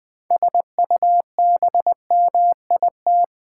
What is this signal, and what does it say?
Morse code, heard as a keyed tone, about 20 wpm, 700 Hz. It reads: SUBMIT
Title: 6 Letter Words - First 3 Letters Given - 20wpm